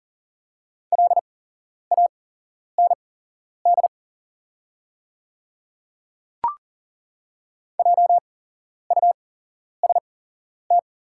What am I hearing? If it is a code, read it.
Morse code, heard as a keyed tone, about 40 wpm, spaced as for 11 wpm, 700 Hz. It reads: LAND JUST
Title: 8x Character Spacing - Top 300 Words - 40wpm